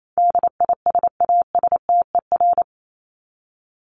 Morse code, heard as a keyed tone, about 28 wpm, 700 Hz. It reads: DISASTER
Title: Top 1501-1750 Words 28wpm